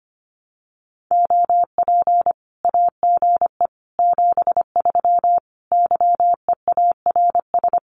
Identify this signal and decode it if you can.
Morse code, heard as a keyed tone, about 25 wpm, 700 Hz. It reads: OP AGE 73 YEARS
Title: Common QSO Phrases 25wpm